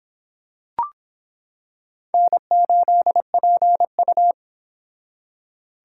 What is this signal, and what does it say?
Morse code, heard as a keyed tone, about 25 wpm, 700 Hz. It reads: N8PU